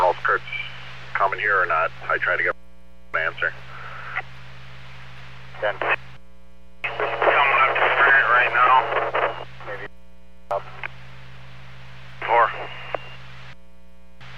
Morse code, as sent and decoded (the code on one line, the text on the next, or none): none